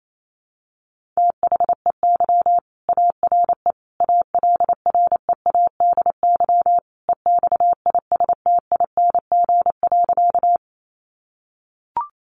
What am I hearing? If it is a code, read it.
Morse code, heard as a keyed tone, about 28 wpm, 700 Hz. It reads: THEY ARE ALREADY EXISTING.
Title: Sentences from Top 1500 Words 28wpm